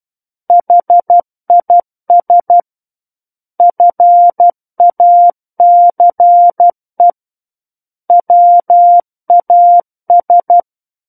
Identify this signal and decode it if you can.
Morse code, heard as a keyed tone, about 12 wpm, 700 Hz. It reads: HIS FACE WAS
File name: Book One / 03-Chapter 3 - 12 WPM